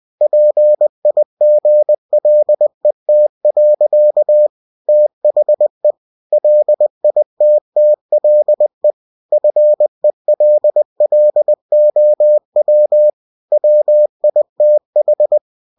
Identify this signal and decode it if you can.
Morse code, heard as a keyed tone, about 20 wpm, 600 Hz. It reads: PIGLET. THE LITTLE FELLOW WITH
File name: cw-books / winnie-the-pooh / winnie-20wpm-page-0105